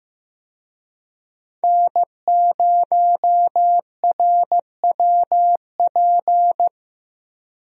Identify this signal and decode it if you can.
Morse code, heard as a keyed tone, about 15 wpm, 700 Hz. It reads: N0RWP
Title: USA Call Signs - 5 Characters - Repeated 2x - 15wpm